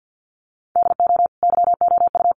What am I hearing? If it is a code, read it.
Morse code, heard as a keyed tone, about 48 wpm, 700 Hz. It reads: DO YOU
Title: Sentences from Top 1400 Words 50wpm